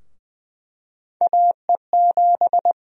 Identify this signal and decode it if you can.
Morse code, heard as a keyed tone, about 20 wpm, 700 Hz. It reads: AE7
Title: USA Call Sign Prefixes - 20wpm